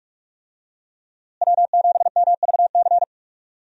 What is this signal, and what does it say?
Morse code, heard as a keyed tone, about 41 wpm, 700 Hz. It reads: W7MVC
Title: USA Call Signs - 5 Characters - Repeated 2x - 45wpm